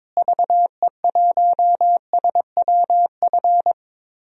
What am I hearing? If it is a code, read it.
Morse code, heard as a keyed tone, about 22 wpm, 700 Hz. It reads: VE1SWF
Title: DXCC-100 Call Signs - 6 Characters - Repeated 2x - 22wpm